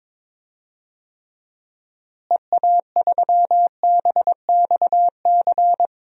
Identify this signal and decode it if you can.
Morse code, heard as a keyed tone, about 22 wpm, 700 Hz. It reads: EA3BXC
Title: DXCC-100 Call Signs - 6 Characters - Repeated 3x - 22wpm